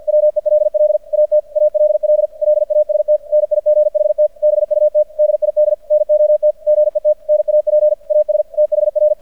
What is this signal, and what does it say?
Morse code, heard as a keyed tone, about 50 wpm, 600 Hz. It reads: WELL AT ALL LAST NIGHT BUT DID NOT GET NAP AS THR
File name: archived / CRSnet